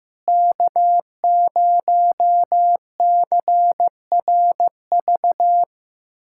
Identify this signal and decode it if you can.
Morse code, heard as a keyed tone, about 16 wpm, 700 Hz. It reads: K0CRV
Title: USA Callsign DE Callsign 15wpm